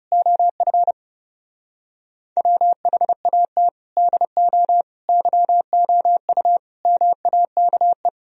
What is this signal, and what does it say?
Morse code, heard as a keyed tone, about 38 wpm, 700 Hz. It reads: OF WHAT DO YOU MAKE